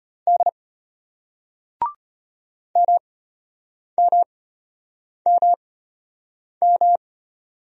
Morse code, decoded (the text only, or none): D MMMM